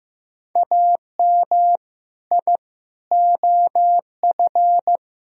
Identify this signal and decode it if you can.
Morse code, heard as a keyed tone, about 16 wpm, 700 Hz. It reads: AM I OF
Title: Sentences from Top 1000 Words - Review Part 3 - 15wpm